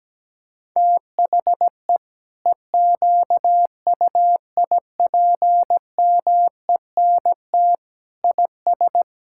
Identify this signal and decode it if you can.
Morse code, heard as a keyed tone, about 17 wpm, 700 Hz. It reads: THE EQUIPMENT IS
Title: Sentences from Top 1000 Words - Review Part 5 - 17wpm